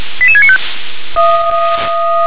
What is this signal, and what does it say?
Five-tone selective call (ZVEI), heard as quick runs of five tones, tones of about 70 ms. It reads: 88695